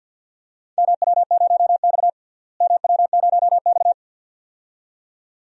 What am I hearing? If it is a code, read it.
Morse code, heard as a keyed tone, about 53 wpm, 700 Hz. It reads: MW0X MW0X
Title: DXCC-100 Call Signs - 4 Characters - Repeated 2x - 50wpm